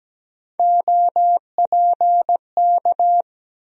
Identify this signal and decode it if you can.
Morse code, heard as a keyed tone, about 17 wpm, 700 Hz. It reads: OPK
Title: USA Call Sign Postfixes - 3 Letters 17wpm